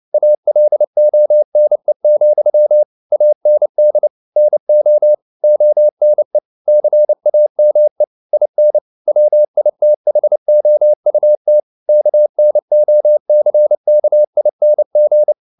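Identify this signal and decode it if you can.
Morse code, heard as a keyed tone, about 29 wpm, 600 Hz. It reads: ALONE, AND NO ONE CAME IN WITHOUT KNOCKING
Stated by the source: Morse code at 29 wpm, sending Ken Follett's novel Never in English, from CW med Gnister.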